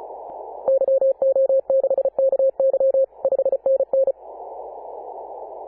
Morse code, heard as a keyed tone, about 34 wpm, 540 Hz. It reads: YO6KY 5NN
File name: Apr/29 2042z JD1BLY 14.011MHz CW